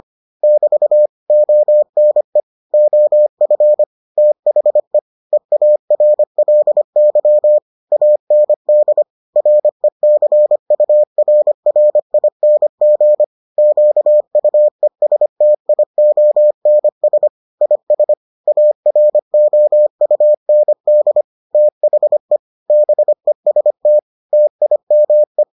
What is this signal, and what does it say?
Morse code, heard as a keyed tone, about 25 wpm, 600 Hz. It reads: <BT> ONE OF THE EARLY AND RECURRING QUESTIONS IS AROUND THE BEST TIME